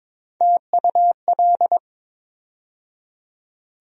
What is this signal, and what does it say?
Morse code, heard as a keyed tone, about 21 wpm, 700 Hz. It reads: TUL
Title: USA Call Sign Postfixes - 3 Letters 22wpm